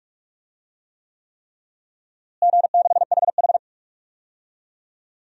Morse code, heard as a keyed tone, about 44 wpm, 700 Hz. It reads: G6HH
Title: DXCC-100 Call Signs - 4 Characters - Repeated 3x - 45wpm